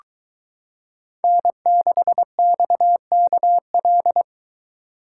Morse code, heard as a keyed tone, about 23 wpm, 700 Hz. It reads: N6XKL